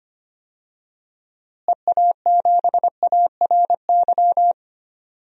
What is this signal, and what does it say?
Morse code, heard as a keyed tone, about 25 wpm, 700 Hz. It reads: EA7ARY
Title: DXCC-100 Call Signs - 6 Characters - Repeated 3x - 25wpm